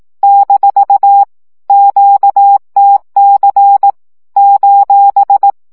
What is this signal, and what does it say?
Morse code, heard as a keyed tone, about 18 wpm, 800 Hz. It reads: - QTC :
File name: morze vizsga 2011-1 15